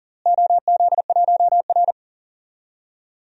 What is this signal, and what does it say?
Morse code, heard as a keyed tone, about 40 wpm, 700 Hz. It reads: OZ1R
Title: DXCC-100 Call Signs - 4 Characters - Repeated 3x - 40wpm